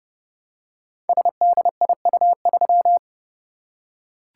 Morse code, heard as a keyed tone, about 30 wpm, 700 Hz. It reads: SDIU3
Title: Code Groups - 5 Letter:Numbers - 30wpm